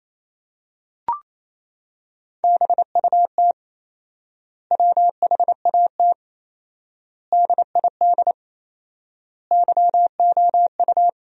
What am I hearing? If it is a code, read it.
Morse code, heard as a keyed tone, about 28 wpm, 700 Hz. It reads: BUT WHAT DID YOU
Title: 4x Word-Spacing - Sentences from Top 100 Words - 28wpm